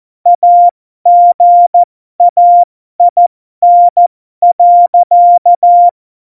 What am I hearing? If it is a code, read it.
Morse code, heard as a keyed tone, about 14 wpm, spaced as for 12 wpm, 700 Hz. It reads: AGAIN.
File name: Morsekurs / Witze